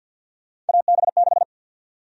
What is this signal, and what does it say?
Morse code, heard as a keyed tone, about 49 wpm, 700 Hz. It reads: AB6
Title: USA Call Sign Prefixes - 50wpm